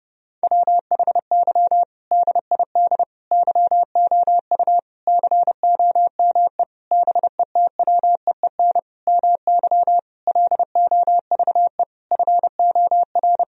Morse code, heard as a keyed tone, about 32 wpm, 700 Hz. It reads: WHY DID YOU COME BETWEEN MY LOVE FOR
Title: Sentences from Top 1000 Words - Review Part 1 - 30wpm